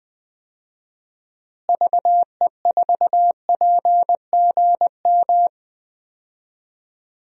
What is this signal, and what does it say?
Morse code, heard as a keyed tone, about 20 wpm, 700 Hz. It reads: VE4PGM
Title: DXCC-100 Call Signs - 6 Characters - Repeated 2x - 20wpm